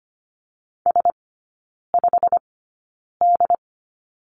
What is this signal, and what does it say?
Morse code, heard as a keyed tone, about 25 wpm, spaced as for 9 wpm, 700 Hz. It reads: S5D
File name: Lesson 15e - Number 5 - Three Characters 25wpm